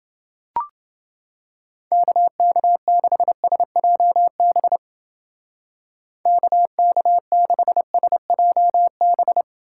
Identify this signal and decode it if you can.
Morse code, heard as a keyed tone, about 29 wpm, 700 Hz. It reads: KK6SJB KK6SJB